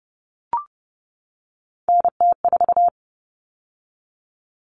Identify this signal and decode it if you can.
Morse code, heard as a keyed tone, about 30 wpm, 700 Hz. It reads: NT4
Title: All Cut Numbers - 3-Digit 30wpm